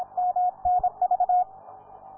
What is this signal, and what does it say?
Morse code, heard as a keyed tone, about 26 wpm, 720 Hz, with radio noise. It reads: MNV